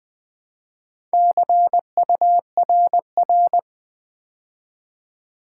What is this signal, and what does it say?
Morse code, heard as a keyed tone, about 20 wpm, 700 Hz. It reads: CURR
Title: Build A Word - 2 Letters At A Time - 8-Letter Words - 20wpm